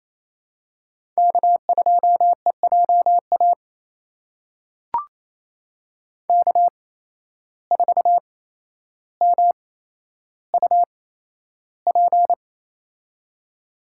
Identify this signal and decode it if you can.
Morse code, heard as a keyed tone, about 28 wpm, spaced as for 12 wpm, 700 Hz. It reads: K2EJA K4MUP
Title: USA Call Signs - 5 Characters - 8x Character Spacing - 28wpm